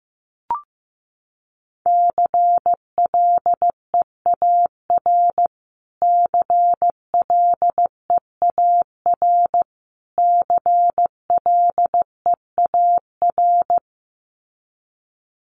Morse code, heard as a keyed tone, about 14 wpm, 700 Hz. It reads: CLEAR CLEAR CLEAR
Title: Top 500 Words Repeated 15wpm